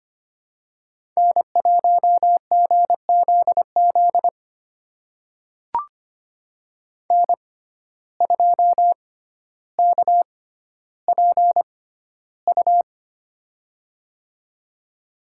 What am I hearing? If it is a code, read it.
Morse code, heard as a keyed tone, about 25 wpm, spaced as for 13 wpm, 700 Hz. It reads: N1GZZ N2KPU
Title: USA Call Signs - 5 Characters - 6x Character Spacing - 25wpm